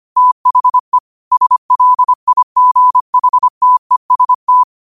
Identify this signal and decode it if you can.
Morse code, heard as a keyed tone, about 25 wpm, 1000 Hz. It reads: THE SLIGHTEST
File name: qotdfiles / 25wpm